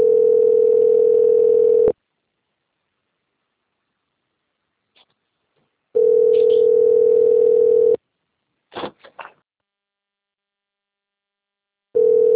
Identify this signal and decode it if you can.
Ringback tone (North America), 440 + 480 Hz.
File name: rg-1600-unknown- - - .42015